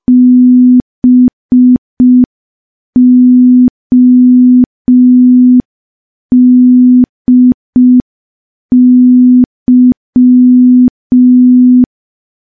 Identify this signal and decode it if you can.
Morse code, heard as a keyed tone, about 5 wpm, 250 Hz. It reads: BODY
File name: morse / morse-training-05WPM